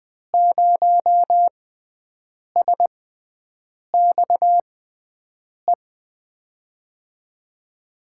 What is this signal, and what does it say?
Morse code, heard as a keyed tone, about 20 wpm, spaced as for 7 wpm, 700 Hz. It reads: 0SXE
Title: USA Call Signs - 6x Character Spacing - 20wpm